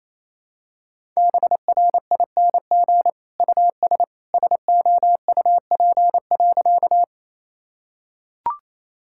Morse code, heard as a keyed tone, about 27 wpm, 700 Hz. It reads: BRING US SOUP.
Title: Sentences from Top 1000 Words - Review Part 6 - 28wpm